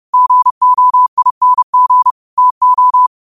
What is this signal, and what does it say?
Morse code, heard as a keyed tone, about 30 wpm, 1000 Hz. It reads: GOING TO
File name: qotdfiles / 30wpm